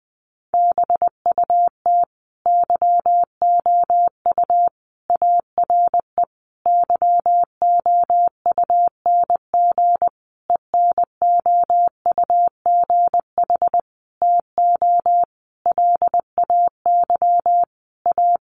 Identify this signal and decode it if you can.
Morse code, heard as a keyed tone, about 20 wpm, 700 Hz. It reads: BUT YOU ARE YOUNG ENOUGH TO LAY A
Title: Sentences from Top 1200 Words 20wpm